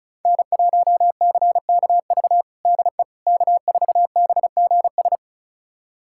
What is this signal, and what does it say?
Morse code, heard as a keyed tone, about 36 wpm, 700 Hz. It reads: N1CKV DE K4BGS
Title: USA Callsign DE Callsign - 5 Characters - 35wpm